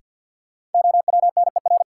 Morse code, heard as a keyed tone, about 49 wpm, 700 Hz. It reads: OWNER
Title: Sentences from Top 1600 Words - New Word Drill Alt - 50wpm